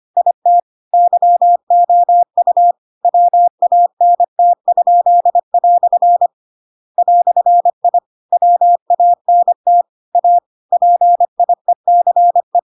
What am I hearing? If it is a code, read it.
Morse code, heard as a keyed tone, about 25 wpm, 700 Hz. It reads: IT YOU WANT?" "I WANT A PIECE
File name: Pinocchio / Pinocchio ch2 25 25 ik1hge